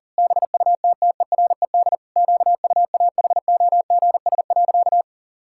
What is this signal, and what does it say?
Morse code, heard as a keyed tone, about 37 wpm, 700 Hz. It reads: BUTTERED QUAHOGS.
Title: Pangram Challenge v2 - 40wpm